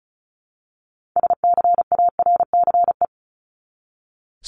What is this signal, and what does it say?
Morse code, heard as a keyed tone, about 34 wpm, 700 Hz. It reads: SCARCE
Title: LY Suffix - Top 100 Words with Base Words 35wpm